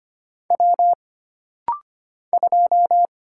Morse code, heard as a keyed tone, about 25 wpm, 700 Hz. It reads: W 2
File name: Single Character (Letter:Number:Punctuation:Prosign) - Rapid-Fire - 25wpm